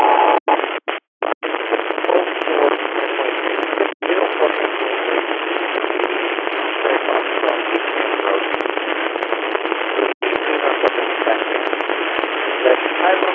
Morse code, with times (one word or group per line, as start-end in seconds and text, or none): none